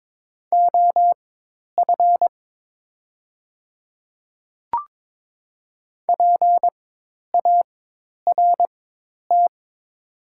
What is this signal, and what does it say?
Morse code, heard as a keyed tone, about 22 wpm, spaced as for 10 wpm, 690 Hz. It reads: OF PART
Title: 4x Character Spacing - Top 300 Words - 22wpm